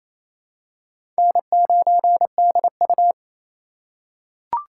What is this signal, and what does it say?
Morse code, heard as a keyed tone, about 28 wpm, 700 Hz. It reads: N9DU
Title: USA Call Signs - 4 Characters - 4x Character Spacing - 28wpm